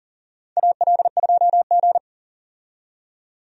Morse code, heard as a keyed tone, about 40 wpm, 700 Hz. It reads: AL2G
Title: USA Call Signs - 4 Characters - 40wpm